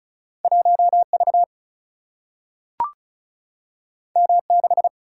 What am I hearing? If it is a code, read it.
Morse code, heard as a keyed tone, about 36 wpm, 700 Hz. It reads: 1 V M 6
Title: Single-Letter Single-Number 35wpm